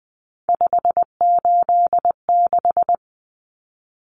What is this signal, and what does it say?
Morse code, heard as a keyed tone, about 20 wpm, 700 Hz. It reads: 586
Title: Common Cut Numbers - 3-Digit 20wpm